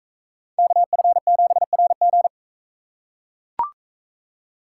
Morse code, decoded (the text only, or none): KF7RG